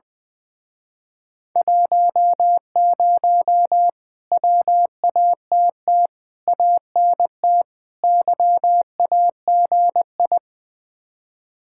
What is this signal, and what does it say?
Morse code, heard as a keyed tone, about 21 wpm, 700 Hz. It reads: 10 WATT ANT YAGI
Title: Speed-Racing Common QSO Phrases - 20wpm